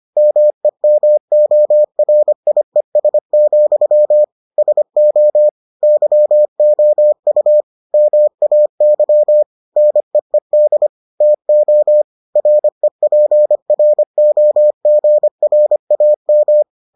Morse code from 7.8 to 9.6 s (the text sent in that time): MAY